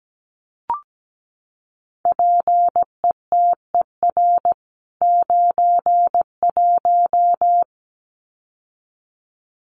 Morse code, heard as a keyed tone, about 17 wpm, 700 Hz. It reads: PETER 91
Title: CWT Exchange 17wpm